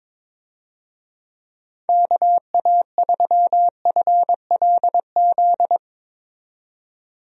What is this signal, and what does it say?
Morse code, heard as a keyed tone, about 22 wpm, 700 Hz. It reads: KA3FLZ
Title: USA Call Signs - 6 Characters - 22wpm